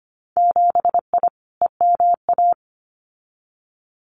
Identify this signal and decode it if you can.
Morse code, heard as a keyed tone, about 25 wpm, 700 Hz. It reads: 7I EMA
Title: Winter Field Day Exchanges Encore 25wpm